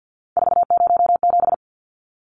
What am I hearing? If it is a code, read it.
Morse code, heard as a keyed tone, about 50 wpm, 700 Hz. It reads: 407
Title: Numbers - 3-Digit 50wpm